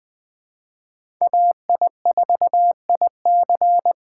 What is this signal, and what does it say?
Morse code, heard as a keyed tone, about 20 wpm, 700 Hz. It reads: AI4IC
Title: USA Call Signs - 5 Characters - 3x Character Spacing - 20wpm